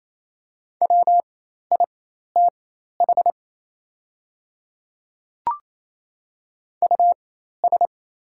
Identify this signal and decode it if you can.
Morse code, heard as a keyed tone, about 28 wpm, spaced as for 13 wpm, 690 Hz. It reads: WITH US
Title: 4x Character Spacing - Top 300 Words - 28wpm